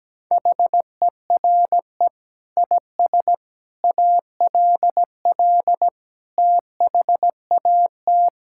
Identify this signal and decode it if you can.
Morse code, heard as a keyed tone, about 17 wpm, 700 Hz. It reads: HERE IS ALL THAT
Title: Sentences from Top 1000 Words - Review Part 6 - 17wpm